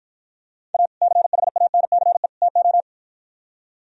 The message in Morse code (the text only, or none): A CHANCE TO